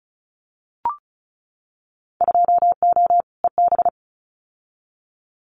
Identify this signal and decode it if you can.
Morse code, heard as a keyed tone, about 36 wpm, 700 Hz. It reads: 2O EB
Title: Winter Field Day Exchanges Encore 35wpm